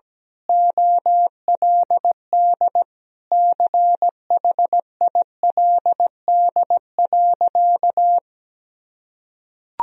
Morse code, heard as a keyed tone, about 17 wpm, 700 Hz. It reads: OLD CHILD.
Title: Sentences from Top 1000 Words - Review Part 2 - 17wpm